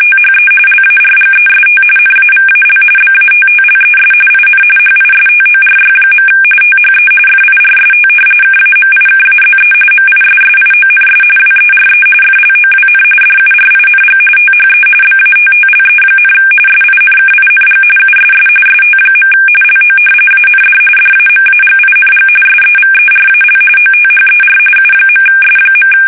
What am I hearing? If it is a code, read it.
RTTY, about 45.45 baud. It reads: THE LARGEST NAVAL BATTLES IN HISTORY. THE FIRST KAMIKAZE MISSIONS WERE FLOWN DURING THE BATTLE, SINKING USS ST. LO AND DAMAGING SEVERAL OTHER U.S.
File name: fsk850